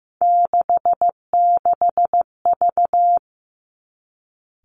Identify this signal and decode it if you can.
Morse code, heard as a keyed tone, about 15 wpm, 700 Hz. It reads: 66V
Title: Three Challenging Letter-Number Combinations 15wpm